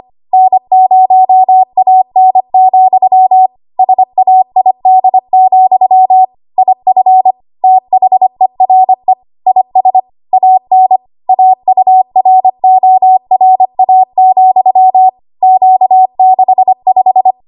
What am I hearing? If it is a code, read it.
Morse code, heard as a keyed tone, about 25 wpm, 750 Hz. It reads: N0AN, SAID, IF THERE IS AN AURORA, Q65
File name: Morse / 25WPM